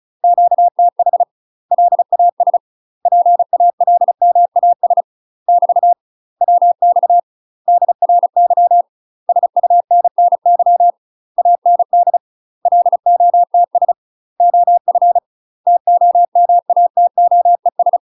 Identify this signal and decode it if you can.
Morse code, heard as a keyed tone, about 36 wpm, 700 Hz. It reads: QTH LAS PALMAS <BT> WX DRY SUNNY AND LOTS OF TOMATOES
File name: UK MASTER QSOs TRAINING / 35WPM / UK MASTER QSOs 0020